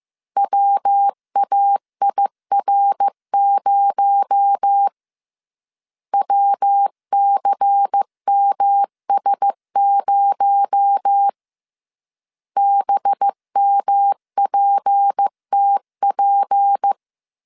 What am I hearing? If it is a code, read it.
Morse code, heard as a keyed tone, about 15 wpm, 800 Hz. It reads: WAIR0 WCMS0 BMPTP
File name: CW / DET / 1 6 1500C S C 14wpm 2951